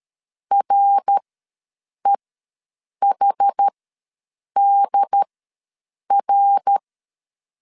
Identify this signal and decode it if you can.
Morse code, heard as a keyed tone, about 13 wpm, spaced as for 7 wpm, 800 Hz. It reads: REHDR
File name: LEZIONI / 5 LBVC / 5 L ws 12wpm 2958